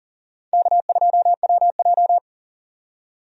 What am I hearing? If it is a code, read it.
Morse code, heard as a keyed tone, about 38 wpm, 700 Hz. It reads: K2WJ
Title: USA Call Signs - 4 Characters - 40wpm